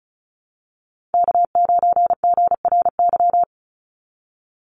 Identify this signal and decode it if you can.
Morse code, heard as a keyed tone, about 36 wpm, 700 Hz. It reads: K9GRY
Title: USA Call Signs 35wpm